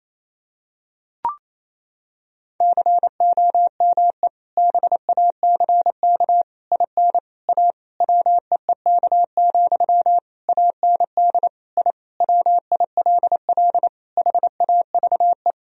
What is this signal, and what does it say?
Morse code, heard as a keyed tone, about 28 wpm, 700 Hz. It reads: COME BACK IN A WEEK, AND I WILL HAVE
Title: Sentences from Top 1000 Words - Review Part 3 - 28wpm